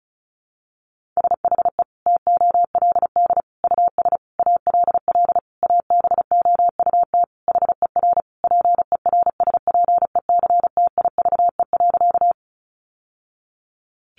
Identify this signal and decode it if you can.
Morse code, heard as a keyed tone, about 34 wpm, 700 Hz. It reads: SHE TOLD US ALL ABOUT HER PERSPECTIVE.
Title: Sentences from Top 1200 Words 35wpm